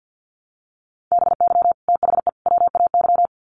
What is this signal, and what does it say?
Morse code, heard as a keyed tone, about 50 wpm, 700 Hz. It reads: BY THE WAY
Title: Sets of 3 Words 50wpm